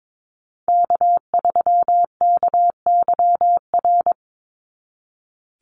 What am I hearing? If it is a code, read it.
Morse code, heard as a keyed tone, about 22 wpm, 700 Hz. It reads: K3KYR
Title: NA SOTA Chaser Callsigns 22wpm